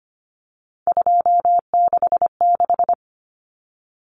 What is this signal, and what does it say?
Morse code, heard as a keyed tone, about 25 wpm, 700 Hz. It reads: 266
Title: Common Cut Numbers - 3-Digit 25wpm